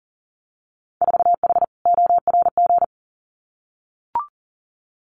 Morse code, heard as a keyed tone, about 41 wpm, 700 Hz. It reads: 4H ORG
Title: Winter Field Day Exchanges Encore 40wpm